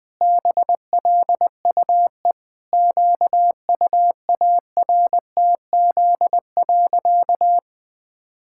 Morse code, heard as a keyed tone, about 20 wpm, 700 Hz. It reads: BLUE QUARTZ.
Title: Pangram Challenge v2 - 20wpm